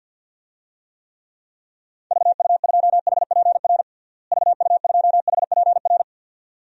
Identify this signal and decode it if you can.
Morse code, heard as a keyed tone, about 52 wpm, 700 Hz. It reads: VU2HPR VU2HPR
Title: DXCC-100 Call Signs - 6 Characters - Repeated 2x - 50wpm